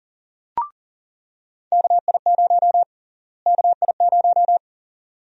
Morse code, heard as a keyed tone, about 39 wpm, 700 Hz. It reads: KI0 KI0
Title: USA Call Sign Prefixes - Repeated 2x - 40wpm